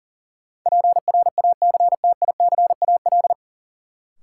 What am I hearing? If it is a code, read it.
Morse code, heard as a keyed tone, about 40 wpm, 700 Hz. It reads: PRACTICAL
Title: Sentences from Top 1400 Words - New Word Drill Alt - 40wpm